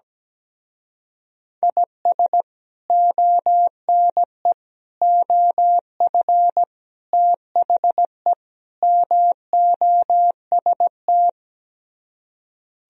Morse code, heard as a keyed tone, about 17 wpm, 700 Hz. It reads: IS ONE OF THE MOST
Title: Speed-Racing - Sets of 5 Words 17wpm